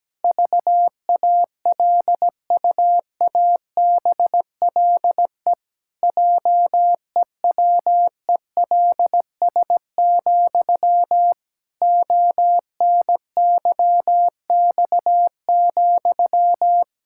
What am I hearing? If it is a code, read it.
Morse code, heard as a keyed tone, about 17 wpm, 700 Hz. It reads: VALUABLE JEWELS, ONYX,
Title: Pangram Challenge v2 - 17wpm